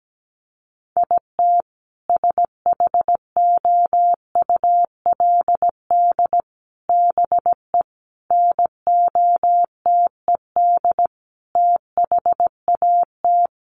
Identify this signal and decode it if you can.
Morse code, heard as a keyed tone, about 17 wpm, 700 Hz. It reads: IT SHOULD BE NOTED THAT
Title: Sets of 5 Words 17wpm